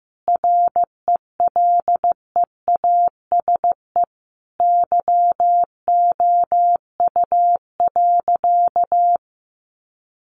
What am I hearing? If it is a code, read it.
Morse code, heard as a keyed tone, about 15 wpm, 700 Hz. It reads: RELEASE YOU.
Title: Sentences from Top 1100 Words 15wpm